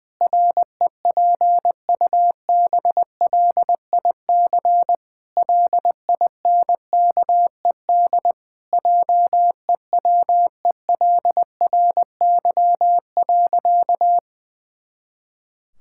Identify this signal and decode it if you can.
Morse code, heard as a keyed tone, about 20 wpm, 700 Hz. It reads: REPUBLIC LINKED JEWELRY.
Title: Pangram Challenge v2 - 20wpm